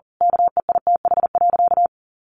Morse code, heard as a keyed tone, about 40 wpm, 700 Hz. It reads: KEITH.
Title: A Sound Of Thunder 40wpm